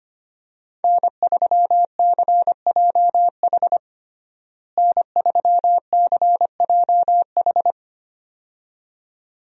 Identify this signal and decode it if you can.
Morse code, heard as a keyed tone, about 25 wpm, 700 Hz. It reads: N3CJH N3CJH
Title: USA Call Signs - 5 Characters - Repeated 2x - 25wpm